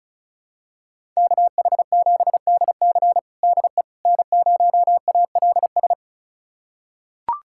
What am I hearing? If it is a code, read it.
Morse code, heard as a keyed tone, about 36 wpm, 700 Hz. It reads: KH7DC DE N0ALS
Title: USA Callsign DE Callsign - 5 Characters - 35wpm